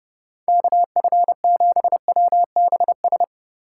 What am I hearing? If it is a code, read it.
Morse code, heard as a keyed tone, about 30 wpm, 700 Hz. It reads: KF7WBS